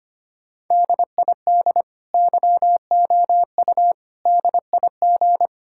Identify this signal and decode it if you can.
Morse code, heard as a keyed tone, about 25 wpm, 700 Hz. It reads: DID YOU DIG
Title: Sentences from Top 1600 Words 25wpm